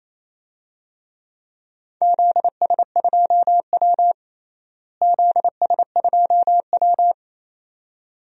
Morse code, heard as a keyed tone, about 28 wpm, 700 Hz. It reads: ZS2W ZS2W
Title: DXCC-100 Call Signs - 4 Characters - Repeated 2x - 28wpm